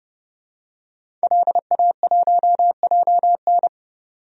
Morse code, heard as a keyed tone, about 29 wpm, 700 Hz. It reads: LA1JN
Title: DXCC-100 Call Signs - 5 Characters - 30wpm